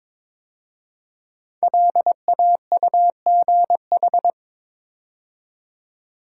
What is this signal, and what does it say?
Morse code, heard as a keyed tone, about 22 wpm, 700 Hz. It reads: LAUGH
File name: 5 Letter Words - First 3 Letters Given - 22wpm